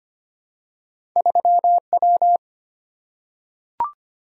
Morse code, heard as a keyed tone, about 25 wpm, 700 Hz. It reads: 3W
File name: Two Letter-Number Combinations - 25wpm